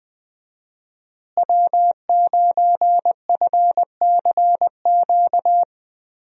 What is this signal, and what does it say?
Morse code, heard as a keyed tone, about 20 wpm, 700 Hz. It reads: W9FCQ
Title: USA Call Signs - 5 Characters - Repeated 3x - 20wpm